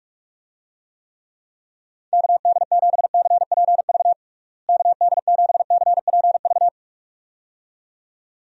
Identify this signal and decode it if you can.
Morse code, heard as a keyed tone, about 41 wpm, 700 Hz. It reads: KD7CPV KD7CPV
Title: USA Call Signs - 6 Characters - Repeated 2x - 45wpm